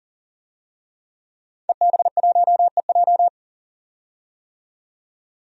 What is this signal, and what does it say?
Morse code, heard as a keyed tone, about 40 wpm, 700 Hz. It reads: EB1EJ
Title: DXCC-100 Call Signs - 5 Characters - 40wpm